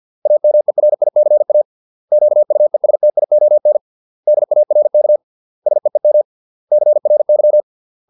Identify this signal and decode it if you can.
Morse code, heard as a keyed tone, about 50 wpm, 600 Hz. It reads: AMERICA QUESTION DARK SEEM CRY